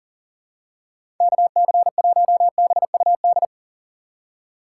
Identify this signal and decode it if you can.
Morse code, heard as a keyed tone, about 40 wpm, 700 Hz. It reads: KC1BUD
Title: USA Call Signs - 6 Characters - 6x Character Spacing - 40wpm